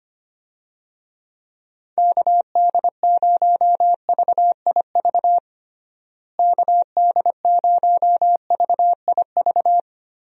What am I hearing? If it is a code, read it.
Morse code, heard as a keyed tone, about 25 wpm, 700 Hz. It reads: KD0VIV KD0VIV
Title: USA Call Signs - 6 Characters - Repeated 3x - 25wpm